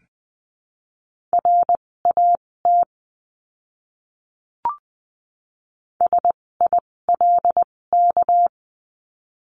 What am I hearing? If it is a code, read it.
Morse code, heard as a keyed tone, about 20 wpm, spaced as for 16 wpm, 700 Hz. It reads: RAT SILK
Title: 256 Lesson 42a - Words 12-20wpm